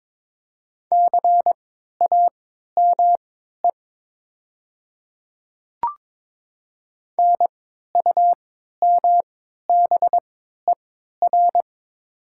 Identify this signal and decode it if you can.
Morse code, heard as a keyed tone, about 22 wpm, spaced as for 12 wpm, 700 Hz. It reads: CAME NUMBER
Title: 3x Character Spacing - Top 200 Words - 22wpm